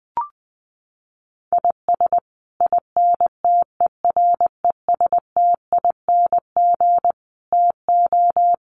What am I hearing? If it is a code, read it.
Morse code, heard as a keyed tone, about 20 wpm, 700 Hz. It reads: IS INTERESTING TO
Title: Sets of 5 Words Encore2 20wpm